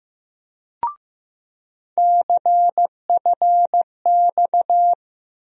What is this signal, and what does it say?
Morse code, heard as a keyed tone, about 15 wpm, 700 Hz. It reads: CFX
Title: USA Call Sign Postfixes - 3 Letters 15wpm